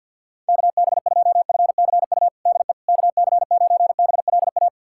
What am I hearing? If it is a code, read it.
Morse code, heard as a keyed tone, about 48 wpm, 700 Hz. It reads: KB2FCU DE KC9BLA
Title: USA Callsign DE Callsign 50wpm